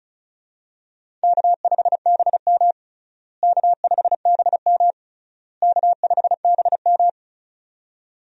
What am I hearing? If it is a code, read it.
Morse code, heard as a keyed tone, about 34 wpm, 700 Hz. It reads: K5BM K5BM K5BM
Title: USA Call Signs - 4 Characters - Repeated 3x - 35wpm